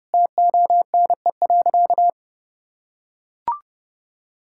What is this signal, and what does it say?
Morse code, heard as a keyed tone, about 32 wpm, 700 Hz. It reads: TONE.
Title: Sentences from Top 1600 Words 30wpm